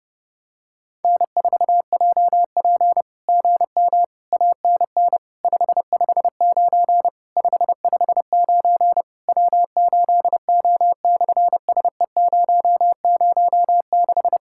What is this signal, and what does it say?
Morse code, heard as a keyed tone, about 30 wpm, 690 Hz. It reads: N4JP GM ANN 559 559 W8O/SE006
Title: SOTA QSO Practice 30wpm